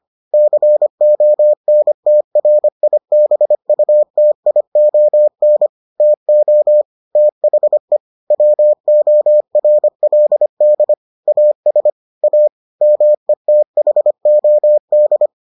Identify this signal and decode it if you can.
Morse code, heard as a keyed tone, about 25 wpm, 600 Hz. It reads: CONTRIBUTION TO THE WORLD AS A METHOD